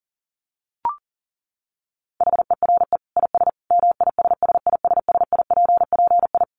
Morse code, heard as a keyed tone, about 38 wpm, 700 Hz. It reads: HERE IS MISSISSIPPI
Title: Dit Challenge 40wpm